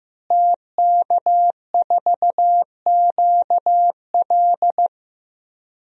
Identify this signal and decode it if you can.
Morse code, heard as a keyed tone, about 15 wpm, 700 Hz. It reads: TK4QL
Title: DXCC-100 Call Signs - 5 Characters - Repeated 3x - 15wpm